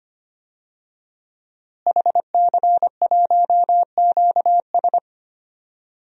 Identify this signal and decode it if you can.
Morse code, heard as a keyed tone, about 25 wpm, 700 Hz. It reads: HC1QS
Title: DXCC-100 Call Signs - 5 Characters - Repeated 3x - 25wpm